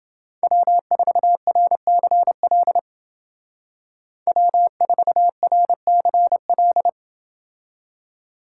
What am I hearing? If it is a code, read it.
Morse code, heard as a keyed tone, about 28 wpm, 700 Hz. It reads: W4RCL W4RCL